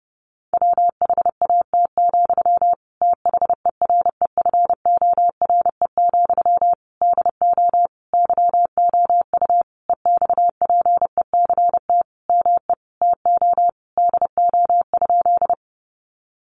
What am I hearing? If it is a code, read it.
Morse code, heard as a keyed tone, about 30 wpm, 700 Hz. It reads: WHAT, THEREFORE, DO YOU EXPECT ME TO DO?
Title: Sentences from Top 1100 Words 30wpm